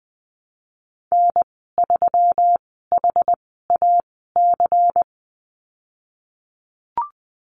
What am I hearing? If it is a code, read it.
Morse code, heard as a keyed tone, about 20 wpm, spaced as for 15 wpm, 700 Hz. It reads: N3HAC
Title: 147 Lesson 24g - Letter y - Call Signs 20wpm